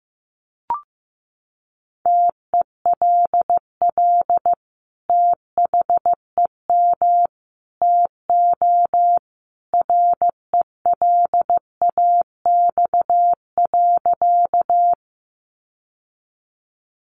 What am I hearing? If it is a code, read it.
Morse code, heard as a keyed tone, about 15 wpm, 700 Hz. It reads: TELL THEM TO RELAX.
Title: Sentences from Top 1400 Words 15wpm